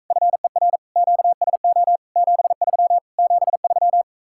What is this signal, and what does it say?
Morse code, heard as a keyed tone, about 45 wpm, 700 Hz. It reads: FER QSO 73 73